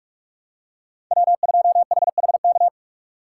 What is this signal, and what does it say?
Morse code, heard as a keyed tone, about 44 wpm, 700 Hz. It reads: W2HHK
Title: USA Call Signs - 6x Character Spacing - 45wpm